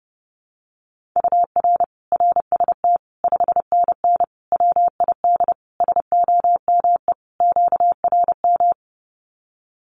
Morse code, heard as a keyed tone, about 30 wpm, 700 Hz. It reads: UR RST 5NN WID SOME QRM
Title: Common QSO Phrases 30wpm